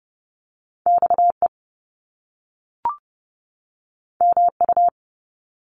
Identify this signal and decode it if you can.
Morse code, heard as a keyed tone, about 30 wpm, 700 Hz. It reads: XE MU
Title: Top Two Letter Combinations in Words 30wpm